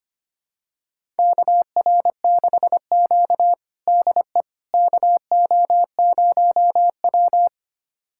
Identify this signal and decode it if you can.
Morse code, heard as a keyed tone, about 25 wpm, 700 Hz. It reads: KR6Q DE KO0W
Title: USA Callsign DE Callsign - 4 Characters - 25wpm